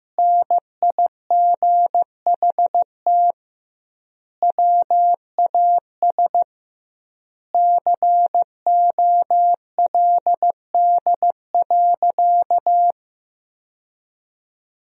Morse code, heard as a keyed tone, about 16 wpm, 700 Hz. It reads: NIGHT WAS COLD.
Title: 2x Word-Spacing - Sentences from Top 400 Words - 15wpm